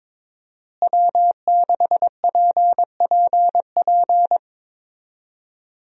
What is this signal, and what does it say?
Morse code, heard as a keyed tone, about 22 wpm, 700 Hz. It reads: W6PPP
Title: USA Call Signs - 5 Characters - Repeated 2x - 22wpm